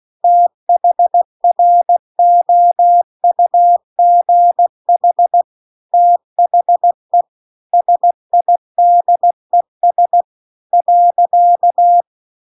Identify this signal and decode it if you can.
Morse code, heard as a keyed tone, about 16 wpm, 700 Hz. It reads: THROUGH THE SIDES .
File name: CW-The-Big-Twoheaded-River-2-16WPM